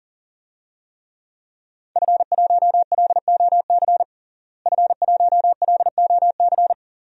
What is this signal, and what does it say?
Morse code, heard as a keyed tone, about 37 wpm, 700 Hz. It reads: F1LOC F1LOC
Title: DXCC-100 Call Signs - 5 Characters - Repeated 3x - 40wpm